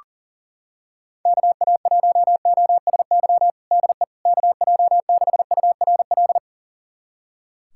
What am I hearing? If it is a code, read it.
Morse code, heard as a keyed tone, about 40 wpm, 700 Hz. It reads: KA1OSY DE KJ6URL
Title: USA Callsign DE Callsign - 6 Characters - 40wpm